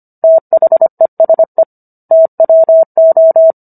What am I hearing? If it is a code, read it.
Morse code, heard as a keyed tone, about 25 wpm, 650 Hz. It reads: THESE TWO